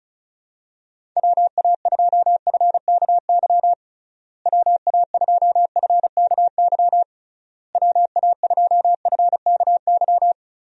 Morse code, heard as a keyed tone, about 34 wpm, 700 Hz. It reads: WA2FKY WA2FKY WA2FKY
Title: USA Call Signs - 6 Characters - Repeated 3x - 35wpm